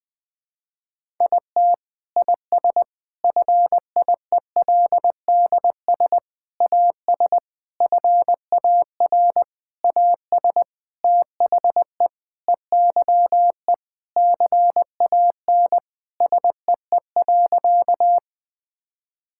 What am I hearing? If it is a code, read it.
Morse code, heard as a keyed tone, about 20 wpm, 700 Hz. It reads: IT IS FIELDS AS FAR AS THE EYE CAN SEE.
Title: Sentences with 7-Word Phrases - 20wpm